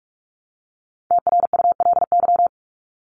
Morse code, heard as a keyed tone, about 44 wpm, 700 Hz. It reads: TRULY
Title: LY Suffix - Top 100 Words with Base Words 45wpm